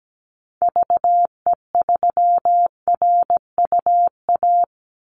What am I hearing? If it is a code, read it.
Morse code, heard as a keyed tone, about 17 wpm, 700 Hz. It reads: VE3RUA
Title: Call Signs Repeated 2x - 17wpm